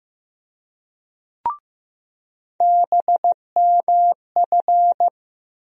Morse code, heard as a keyed tone, about 15 wpm, 700 Hz. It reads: BMF
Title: USA Call Sign Postfixes - 3 Letters 15wpm